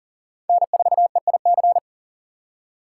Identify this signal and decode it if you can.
Morse code, heard as a keyed tone, about 39 wpm, 700 Hz. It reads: N4EIC
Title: USA Call Signs - 5 Characters - 6x Character Spacing - 40wpm